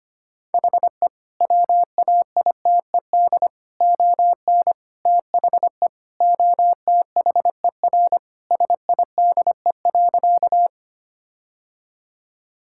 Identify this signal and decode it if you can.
Morse code, heard as a keyed tone, about 25 wpm, 700 Hz. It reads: HE WAITED ON THE OTHER SIDE.
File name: practice / 25wpm / Sentences with 4-Word Phrases - 25wpm